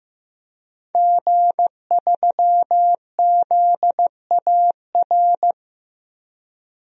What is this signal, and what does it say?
Morse code, heard as a keyed tone, about 15 wpm, 700 Hz. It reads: G3ZAR
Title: DXCC-100 Call Signs - 5 Characters - 15wpm